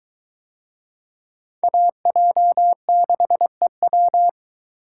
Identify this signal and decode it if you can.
Morse code, heard as a keyed tone, about 21 wpm, 700 Hz. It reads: AJ6EW